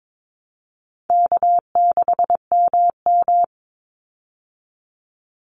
Morse code, heard as a keyed tone, about 22 wpm, 700 Hz. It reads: K6MM
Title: Easy Call Signs Repeated 2x - 22wpm